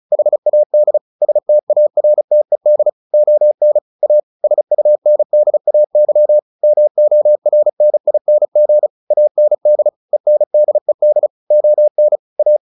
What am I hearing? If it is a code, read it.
Morse code, heard as a keyed tone, about 35 wpm, 600 Hz. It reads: HAD STARTED ON A SUNDAY MORNING AND ENDED ON A
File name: the old man and the sea 2 35wpm